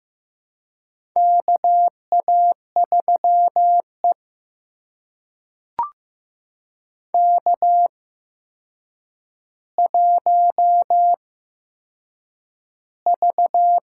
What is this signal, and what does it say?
Morse code, heard as a keyed tone, about 15 wpm, spaced as for 7 wpm, 700 Hz. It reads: KA3E K1V
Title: USA Call Signs - 4 Characters - 8x Character Spacing - 15wpm